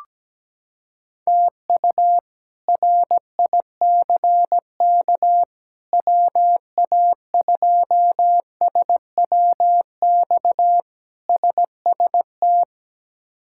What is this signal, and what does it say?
Morse code, heard as a keyed tone, about 17 wpm, 700 Hz. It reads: TU RICK WA2SWX SST
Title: K1USN SST QSO Practice - 17wpm